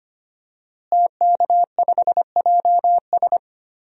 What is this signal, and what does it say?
Morse code, heard as a keyed tone, about 25 wpm, 700 Hz. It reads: TK5JS
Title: DXCC-100 Call Signs - 5 Characters - Repeated 3x - 25wpm